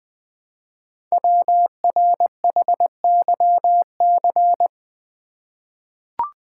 Morse code, heard as a keyed tone, about 20 wpm, 700 Hz. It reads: WRHYC
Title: Code Groups - 5 Letters - 20wpm